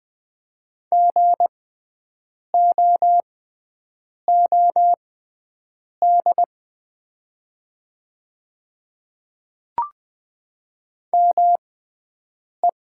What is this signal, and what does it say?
Morse code, heard as a keyed tone, about 20 wpm, spaced as for 7 wpm, 700 Hz. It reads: GOOD ME
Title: 6x Character Spacing - Top 200 Words - 20wpm